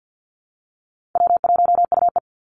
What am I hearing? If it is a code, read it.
Morse code, heard as a keyed tone, about 52 wpm, 700 Hz. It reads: W1UE
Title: Call Signs - 50wpm